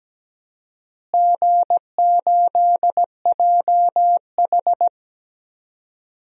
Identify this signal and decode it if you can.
Morse code, heard as a keyed tone, about 17 wpm, 700 Hz. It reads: G8JH
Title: DXCC-100 Call Signs - 4 Characters - Repeated 3x - 17wpm